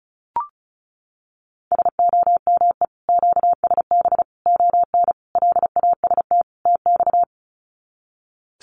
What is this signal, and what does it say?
Morse code, heard as a keyed tone, about 34 wpm, 700 Hz. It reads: SOME QSB ON LAST TX
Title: Common QSO Phrases 35wpm